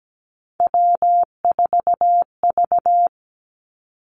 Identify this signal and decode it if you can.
Morse code, heard as a keyed tone, about 17 wpm, 700 Hz. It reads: W4V
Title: Three Challenging Letter-Number Combinations 17wpm